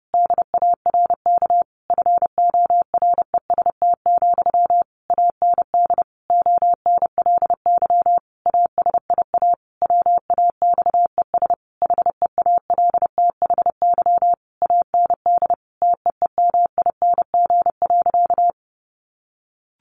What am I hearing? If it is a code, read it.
Morse code, heard as a keyed tone, about 30 wpm, 700 Hz. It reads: DARK FOREST, AND ONLY ASIA WAXES HEALTHY AND TEEMING.
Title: A Sound Of Thunder 30wpm